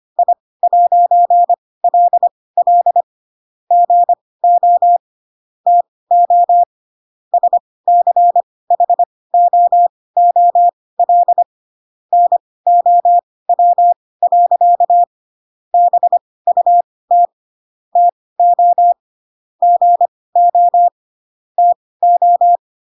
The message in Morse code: I'LL GO TO SCHOOL NOW. BUT TO GO TO

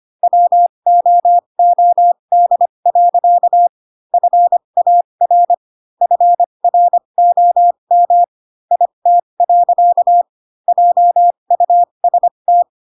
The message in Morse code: WOOD. FAR FROM IT. JUST

